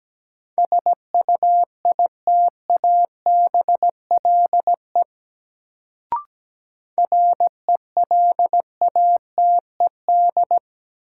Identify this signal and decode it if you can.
Morse code, heard as a keyed tone, about 17 wpm, 700 Hz. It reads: SUITABLE RELATED
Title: Sentences from Top 1400 Words - New Word Drill Alt - 17wpm